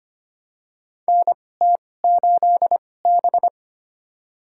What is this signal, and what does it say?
Morse code, heard as a keyed tone, about 25 wpm, spaced as for 18 wpm, 700 Hz. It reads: NT8B
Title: USA Call Signs - 4 Characters - 2x Character Spacing - 25wpm